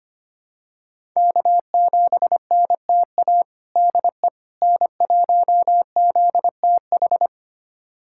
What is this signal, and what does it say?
Morse code, heard as a keyed tone, about 25 wpm, 700 Hz. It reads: K7NTA DE N1ZTH
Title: USA Callsign DE Callsign - 5 Characters - 25wpm